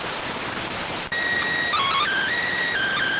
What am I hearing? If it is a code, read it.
SSTV (slow-scan television), PD 290.